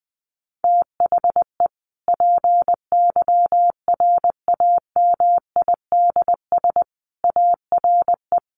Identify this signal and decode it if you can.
Morse code, heard as a keyed tone, about 20 wpm, 700 Hz. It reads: THE PYRAMIDS ARE
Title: A Sound Of Thunder 20wpm